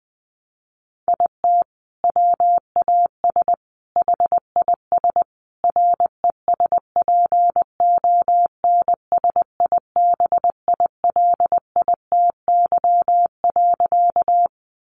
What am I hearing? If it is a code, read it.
Morse code, heard as a keyed tone, about 20 wpm, 700 Hz. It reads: IT WAS HIS RESPONSIBILITY.
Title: Sentences from Top 1200 Words 20wpm